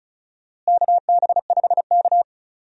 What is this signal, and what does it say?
Morse code, heard as a keyed tone, about 34 wpm, 700 Hz. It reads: KB5K
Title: USA Call Signs - 4 Characters - Repeated 3x - 35wpm